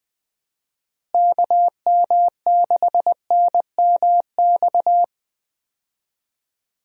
Morse code, heard as a keyed tone, about 20 wpm, 700 Hz. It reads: KM6NMX
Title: USA Call Signs - 6 Characters - Repeated 2x - 20wpm